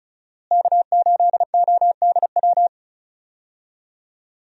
Morse code, heard as a keyed tone, about 36 wpm, 700 Hz. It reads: K8ODW
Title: USA Call Signs - 5 Characters - Repeated 3x - 35wpm